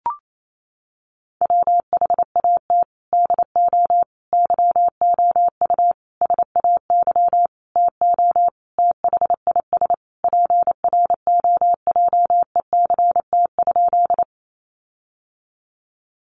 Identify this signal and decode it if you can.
Morse code, heard as a keyed tone, about 28 wpm, 700 Hz. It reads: WHAT DO YOU SAY TO THIS PROJECT?
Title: Sentences from Top 1400 Words 28wpm